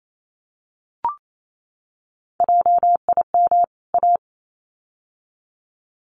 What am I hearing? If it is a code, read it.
Morse code, heard as a keyed tone, about 28 wpm, 700 Hz. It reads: JIM A
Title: CWT Exchange 28wpm